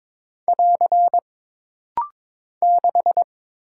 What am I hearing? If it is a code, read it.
Morse code, heard as a keyed tone, about 22 wpm, 700 Hz. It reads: <AR> 6
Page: This rapid-fire timing is geared to push your ICR (Instant Character Recognition) skills.